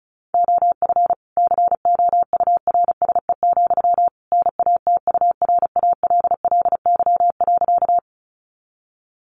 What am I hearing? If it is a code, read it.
Morse code, heard as a keyed tone, about 34 wpm, 700 Hz. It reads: OF COURSE, NATURALLY.
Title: Sentences from Top 1300 Words 35wpm